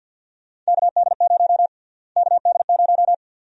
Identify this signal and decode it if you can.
Morse code, heard as a keyed tone, about 53 wpm, 700 Hz. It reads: KD0 KD0
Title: USA Call Sign Prefixes - Repeated 2x - 50wpm